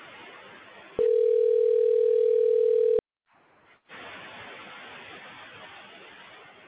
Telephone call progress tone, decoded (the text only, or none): ringback tone